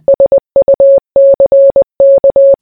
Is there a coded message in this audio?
Yes: Morse code from 0.1 to 2.5 s, heard as a keyed tone, about 20 wpm, 550 Hz. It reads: SUCK